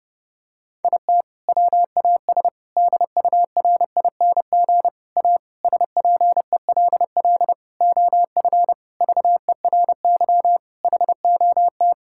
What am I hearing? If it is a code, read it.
Morse code, heard as a keyed tone, about 32 wpm, 700 Hz. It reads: IT WAS DURING A SPELL OF VERY HOT
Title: Sentences from Top 1000 Words - Review Part 5 - 30wpm